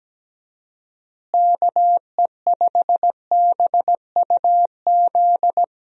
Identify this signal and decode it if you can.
Morse code, heard as a keyed tone, about 17 wpm, 700 Hz. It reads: KE5BUZ
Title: USA Call Signs - 6 Characters - Repeated 2x - 17wpm